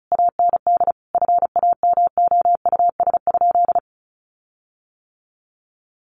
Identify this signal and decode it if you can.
Morse code, heard as a keyed tone, about 34 wpm, 700 Hz. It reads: AND FAMOUS?
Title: Sentences from Top 1500 Words 35wpm